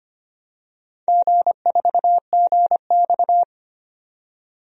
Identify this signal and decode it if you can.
Morse code, heard as a keyed tone, about 26 wpm, 690 Hz. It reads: G4GX
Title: DXCC-100 Call Signs - 4 Characters - 25wpm